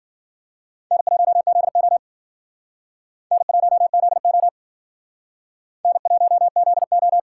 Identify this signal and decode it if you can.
Morse code, heard as a keyed tone, about 50 wpm, 700 Hz. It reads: N1ZO N1ZO N1ZO